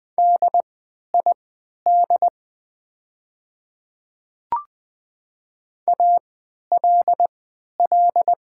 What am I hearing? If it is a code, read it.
Morse code, heard as a keyed tone, about 20 wpm, spaced as for 11 wpm, 700 Hz. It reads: DID ALL
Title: 3x Character Spacing - Top 100 Words - 20wpm